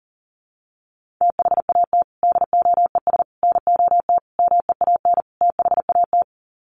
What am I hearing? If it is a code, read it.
Morse code, heard as a keyed tone, about 40 wpm, 700 Hz. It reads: THAT DOES NOT MEAN THAT
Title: Sets of 5 Words Encore 40wpm